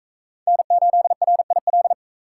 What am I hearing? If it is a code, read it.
Morse code, heard as a keyed tone, about 40 wpm, 700 Hz. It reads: N8RIL